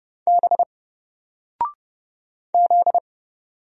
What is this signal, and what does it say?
Morse code, heard as a keyed tone, about 30 wpm, 700 Hz. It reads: B Z
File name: Single-Letter-Number-30wpm